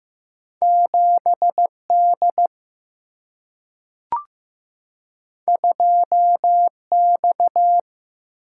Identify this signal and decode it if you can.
Morse code, heard as a keyed tone, about 15 wpm, 700 Hz. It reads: 7 D 2 X
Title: Single-Letter Single-Number 15wpm